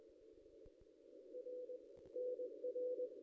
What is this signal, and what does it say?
Morse code, heard as a keyed tone, about 21 wpm, 500 Hz. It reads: R NR
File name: contests / ss-cw-k5tr-op-n5rz -1900